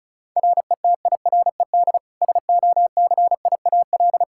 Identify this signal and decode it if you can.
Morse code, heard as a keyed tone, about 36 wpm, 700 Hz. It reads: RETIRED SOCIAL
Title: Practice QSO - Exchange 3 - More Info - 35wpm